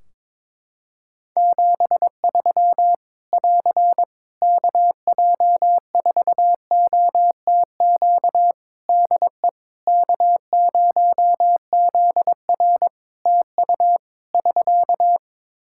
Morse code, heard as a keyed tone, about 21 wpm, 700 Hz. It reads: 73 <AR> KJ4OTQ DE K0ZR TU <SK>
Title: Practice QSO - Exchange 4 - Ending - 22wpm